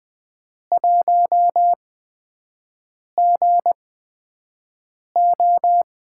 Morse code, heard as a keyed tone, about 20 wpm, spaced as for 5 wpm, 700 Hz. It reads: 1GO
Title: USA Call Signs - 5 Characters - 8x Character Spacing - 20wpm